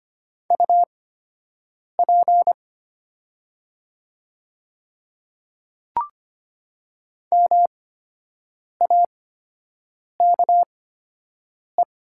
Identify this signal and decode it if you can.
Morse code, heard as a keyed tone, about 25 wpm, spaced as for 7 wpm, 700 Hz. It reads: UP MAKE
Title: 8x Character Spacing - Top 100 Words - 25wpm